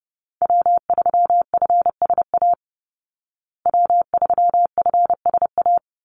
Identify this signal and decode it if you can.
Morse code, heard as a keyed tone, about 30 wpm, 700 Hz. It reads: W3FSA W3FSA
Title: Call Signs Repeated 2x - 30wpm